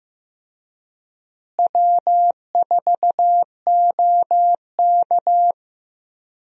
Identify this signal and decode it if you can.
Morse code, heard as a keyed tone, about 15 wpm, 700 Hz. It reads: W4OK
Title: USA Call Signs - 4 Characters - Repeated 3x - 15wpm